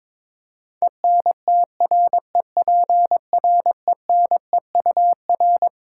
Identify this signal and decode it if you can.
Morse code, heard as a keyed tone, about 22 wpm, 700 Hz. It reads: ENTREPRENEUR
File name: Twelve Letter Words - 22wpm